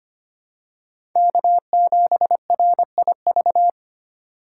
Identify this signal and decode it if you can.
Morse code, heard as a keyed tone, about 25 wpm, 700 Hz. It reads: K7RIV
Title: USA Call Signs - 5 Characters - 2x Character Spacing - 25wpm